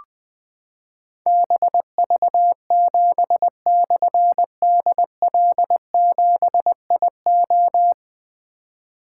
Morse code, heard as a keyed tone, about 20 wpm, 700 Hz. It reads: BV7/DL7IO
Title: Difficult Call Signs 20wpm